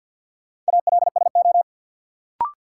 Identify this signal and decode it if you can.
Morse code, heard as a keyed tone, about 53 wpm, 700 Hz. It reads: ALSO
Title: Sentences from Top 400 Words - New Word Drill - 50wpm